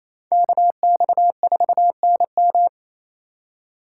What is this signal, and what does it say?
Morse code, heard as a keyed tone, about 28 wpm, 700 Hz. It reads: KX4NM
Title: K1USN SST QSO Practice - 28wpm